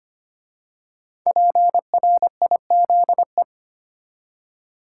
Morse code, heard as a keyed tone, about 25 wpm, 700 Hz. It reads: PRIZE
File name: 5 Letter Words - First 3 Letters Given - 25wpm